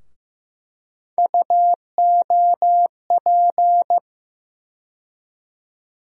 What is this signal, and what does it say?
Morse code, heard as a keyed tone, about 15 wpm, 700 Hz. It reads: UOP
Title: USA Call Sign Postfixes - 3 Letters 15wpm